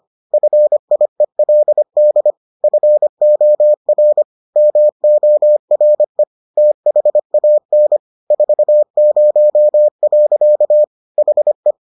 Morse code, heard as a keyed tone, about 25 wpm, 600 Hz. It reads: FIELD FOR MORE THAN 40. HE